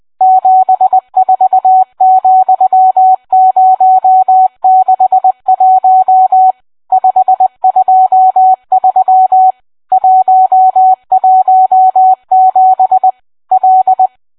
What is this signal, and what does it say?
Morse code, heard as a keyed tone, about 20 wpm, 750 Hz, with radio noise. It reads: 74,061 523 117 L